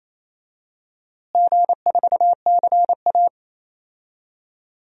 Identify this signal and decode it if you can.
Morse code, heard as a keyed tone, about 28 wpm, 700 Hz. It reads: G4CA
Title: DXCC-100 Call Signs - 4 Characters - Repeated 2x - 28wpm